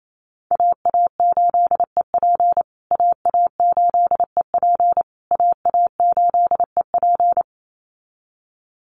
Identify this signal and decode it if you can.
Morse code, heard as a keyed tone, about 28 wpm, 700 Hz. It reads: AA8EP AA8EP AA8EP
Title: USA Call Signs Repeated 3x 28wpm